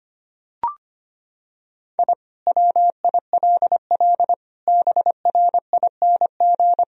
Morse code, heard as a keyed tone, about 25 wpm, 700 Hz. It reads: I WILL BRING
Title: Sentences from Top 1000 Words - Review Part 3 - 25wpm